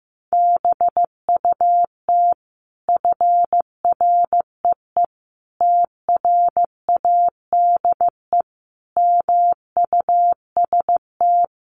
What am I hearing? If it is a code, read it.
Morse code, heard as a keyed tone, about 15 wpm, 700 Hz. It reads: BUT FREE TRADE MUST
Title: Sentences from Top 1300 Words 15wpm